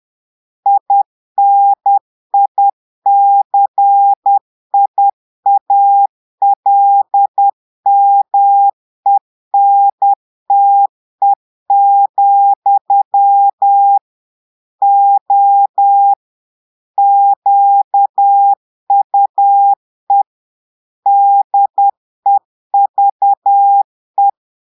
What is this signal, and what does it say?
Morse code, heard as a keyed tone, about 10 wpm, 800 Hz. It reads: INICIALMENTE, O QUE DEVE
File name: portugais texte en clair 2 vitesse 10 mots minutes